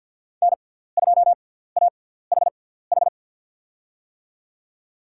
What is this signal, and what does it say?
Morse code, heard as a keyed tone, about 49 wpm, spaced as for 17 wpm, 700 Hz. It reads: N2AHH
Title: USA Call Signs - 6x Character Spacing - 50wpm